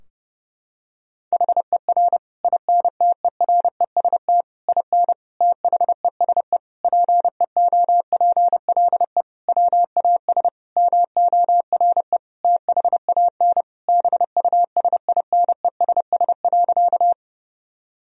Morse code, heard as a keyed tone, about 30 wpm, 700 Hz. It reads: HER INTEREST IN THESE PEOPLE WAS MORE THAN BUSINESS.
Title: Sentences from Top 1000 Words - Review Part 2 - 30wpm